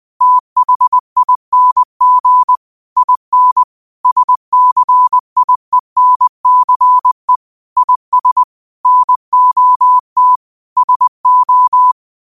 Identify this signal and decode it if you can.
Morse code, heard as a keyed tone, about 20 wpm, 1000 Hz. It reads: THING IN SCIENCE IS NOT SO
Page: 20 WPM morse code quotes for Tue, 12 Nov 2024.